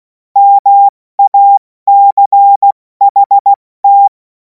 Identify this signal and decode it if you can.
Morse code, heard as a keyed tone, about 16 wpm, 800 Hz. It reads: MACHT
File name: Morsekurs / Witze / Witz 28. April 2026 70 F80